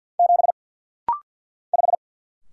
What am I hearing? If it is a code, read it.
Morse code, heard as a keyed tone, about 49 wpm, 690 Hz. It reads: 7 5
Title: Single Letter-Number - Rapid-Fire 50wpm